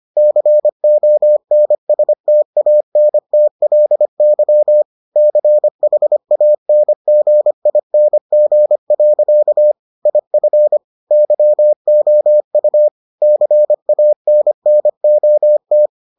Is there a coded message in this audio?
Yes: Morse code from 0.2 to 15.9 s, heard as a keyed tone, about 25 wpm, 600 Hz. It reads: CONSTANTLY CHANGING. IF YOU CANNOT